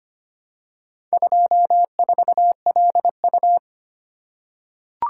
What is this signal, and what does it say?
Morse code, heard as a keyed tone, about 25 wpm, 700 Hz. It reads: 24LU
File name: practice / 25wpm / Four Letter-Number Combinations - 25wpm